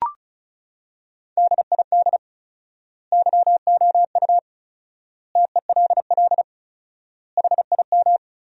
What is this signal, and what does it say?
Morse code, heard as a keyed tone, about 36 wpm, 700 Hz. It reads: DID YOU TELL HIM
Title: 4x Word-Spacing - Sentences from Top 400 Words - 35wpm